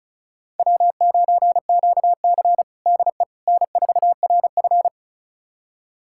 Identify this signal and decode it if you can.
Morse code, heard as a keyed tone, about 34 wpm, 700 Hz. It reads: W9QC DE N4RF
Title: USA Callsign DE Callsign - 4 Characters - 35wpm